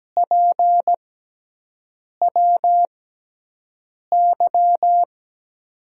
Morse code, heard as a keyed tone, about 17 wpm, spaced as for 6 wpm, 700 Hz. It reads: PWY
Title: USA Call Signs - 5 Characters - 6x Character Spacing - 17wpm